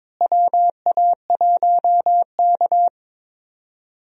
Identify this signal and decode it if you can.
Morse code, heard as a keyed tone, about 22 wpm, 700 Hz. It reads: WA1K
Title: USA Call Signs - 4 Characters - Repeated 3x - 22wpm